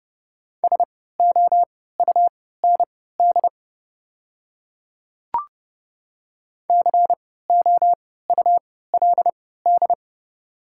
Morse code, heard as a keyed tone, about 30 wpm, spaced as for 17 wpm, 700 Hz. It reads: SOUND COULD
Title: 3x Character Spacing - Top 100 Words - 30wpm